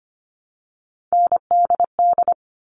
Morse code, heard as a keyed tone, about 25 wpm, 700 Hz. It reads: NDD
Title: All Cut Numbers - 3-Digit 25wpm